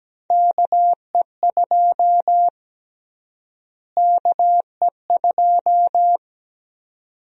Morse code, heard as a keyed tone, about 17 wpm, 700 Hz. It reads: KE2 KE2
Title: USA Call Sign Prefixes - Repeated 2x - 17wpm